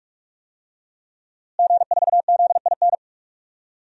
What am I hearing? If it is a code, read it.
Morse code, heard as a keyed tone, about 46 wpm, 700 Hz. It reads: G4ZIN
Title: DXCC-100 Call Signs - 5 Characters - Repeated 2x - 45wpm